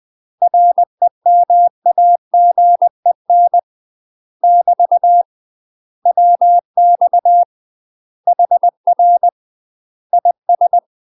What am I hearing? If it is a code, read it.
Morse code, heard as a keyed tone, about 20 wpm, 700 Hz. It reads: REMAGEN <BT> WX HR IS
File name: UK MASTER QSOs TRAINING / 20WPM / UK MASTER QSOs 0004